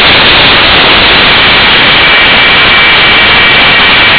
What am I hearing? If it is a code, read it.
PSK31, about 31.25 baud.